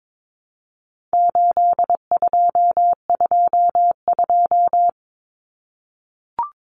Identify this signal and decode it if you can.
Morse code, heard as a keyed tone, about 22 wpm, 700 Hz. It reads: 8222
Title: Common Cut Numbers - 4-Digit 22wpm